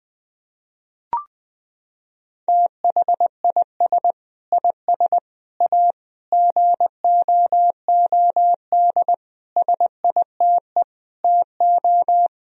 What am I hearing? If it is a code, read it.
Morse code, heard as a keyed tone, about 20 wpm, 700 Hz. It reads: THIS IS A GOOD SITE TO
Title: Sentences from Top 1000 Words - Review Part 4 - 20wpm